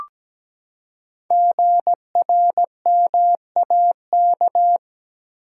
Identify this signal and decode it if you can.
Morse code, heard as a keyed tone, about 17 wpm, 700 Hz. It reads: GRMAK
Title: Code Groups - 5 Letters - 17wpm